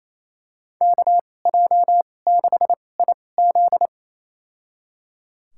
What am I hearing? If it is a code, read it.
Morse code, heard as a keyed tone, about 27 wpm, spaced as for 20 wpm, 700 Hz. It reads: KJ6IZ
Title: USA Call Signs - 5 Characters - 2x Character Spacing - 28wpm